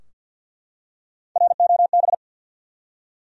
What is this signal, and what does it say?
Morse code, heard as a keyed tone, about 53 wpm, 700 Hz. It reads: ROB
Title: USA Common Names - Max 3 Letters - 50wpm